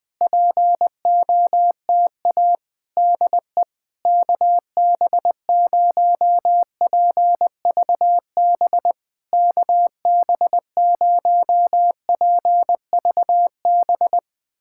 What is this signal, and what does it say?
Morse code, heard as a keyed tone, about 20 wpm, 700 Hz. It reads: POTA DE KB0PVB KB0PVB
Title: POTA QSO Practice - 20wpm